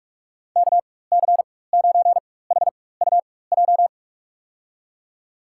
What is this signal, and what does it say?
Morse code, heard as a keyed tone, about 46 wpm, spaced as for 21 wpm, 690 Hz. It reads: KC9HUJ
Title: USA Call Signs - 4x Character Spacing - 45wpm